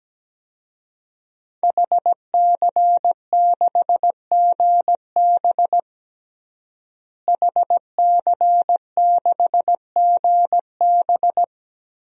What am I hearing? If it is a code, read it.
Morse code, heard as a keyed tone, about 17 wpm, 700 Hz. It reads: HC6GB HC6GB
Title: DXCC-100 Call Signs - 5 Characters - Repeated 2x - 17wpm